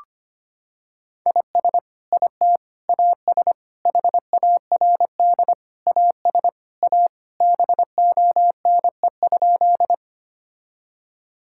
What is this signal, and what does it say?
Morse code, heard as a keyed tone, about 25 wpm, 700 Hz. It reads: IS IT AS HARD AS A BONE?
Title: Sentences from Top 1600 Words 25wpm